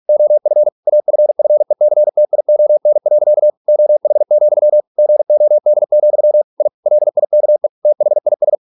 Morse code, heard as a keyed tone, about 46 wpm, 600 Hz. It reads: OF AFFECTION. OH, GOD, I LIKE THIS